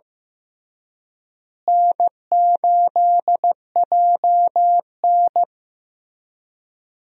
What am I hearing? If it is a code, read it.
Morse code, heard as a keyed tone, about 15 wpm, 700 Hz. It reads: N8JN
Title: USA Call Signs - 4 Characters - Repeated 2x - 15wpm